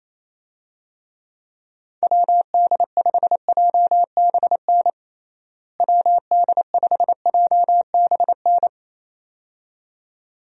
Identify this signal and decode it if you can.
Morse code, heard as a keyed tone, about 27 wpm, 700 Hz. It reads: WD5JBN WD5JBN
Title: USA Call Signs - 6 Characters - Repeated 2x - 28wpm